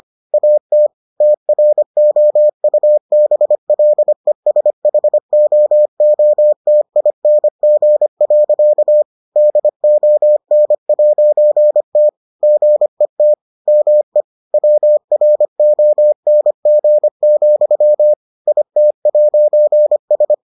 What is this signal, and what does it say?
Morse code, heard as a keyed tone, about 25 wpm, 600 Hz. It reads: AT TROUBLESHOOTING. DON'T GET ME WRONG, IT'S